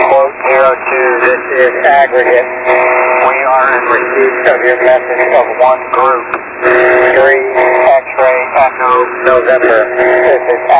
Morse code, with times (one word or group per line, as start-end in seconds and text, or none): none